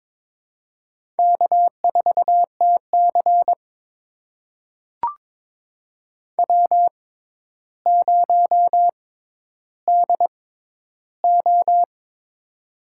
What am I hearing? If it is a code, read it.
Morse code, heard as a keyed tone, about 22 wpm, spaced as for 11 wpm, 700 Hz. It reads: K4TC W0DO
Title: USA Call Signs - 4 Characters - 6x Character Spacing - 22wpm